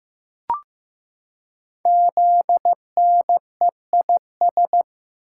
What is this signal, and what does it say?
Morse code, heard as a keyed tone, about 15 wpm, 700 Hz. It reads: ZNEIS
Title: Code Groups - 5 Letters - 15wpm